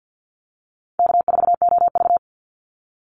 Morse code, heard as a keyed tone, about 48 wpm, 700 Hz. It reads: K4OV
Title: Easy Call Signs Repeated 3x - 50wpm